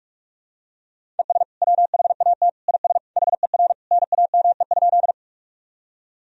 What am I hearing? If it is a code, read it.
Morse code, heard as a keyed tone, about 47 wpm, 700 Hz. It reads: ES WHAT IS HER NAME?
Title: Sentences from Top 200 Words and Abbreviations 45wpm